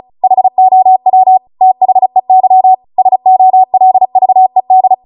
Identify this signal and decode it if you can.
Morse code, heard as a keyed tone, about 36 wpm, 750 Hz. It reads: HOW THEY SOLVED